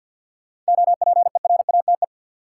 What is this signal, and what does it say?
Morse code, heard as a keyed tone, about 49 wpm, 700 Hz. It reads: OPERATE
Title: Sentences from Top 1500 Words - New Word Drill Alt - 50wpm